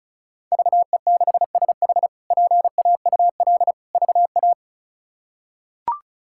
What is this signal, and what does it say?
Morse code, heard as a keyed tone, about 36 wpm, 700 Hz. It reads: VE6SH PAUL VA
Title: K1USN SST QSO Practice - 35wpm